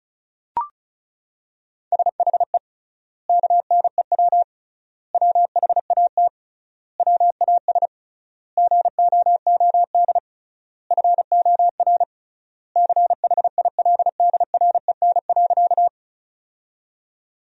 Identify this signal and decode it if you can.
Morse code, heard as a keyed tone, about 34 wpm, 700 Hz. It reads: SHE KNEW WHAT WAS GOOD FOR CHILDREN.
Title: 3x Word-Spacing - Sentences from Top 400 Words - 35wpm